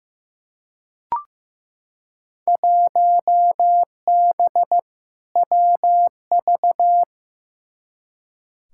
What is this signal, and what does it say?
Morse code, heard as a keyed tone, about 15 wpm, 700 Hz. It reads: 1B WV
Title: Field Day Exchanges - 15wpm